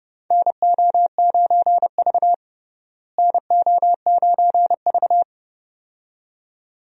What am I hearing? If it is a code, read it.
Morse code, heard as a keyed tone, about 29 wpm, 700 Hz. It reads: NO9V NO9V
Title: USA Call Signs - 4 Characters - Repeated 2x - 30wpm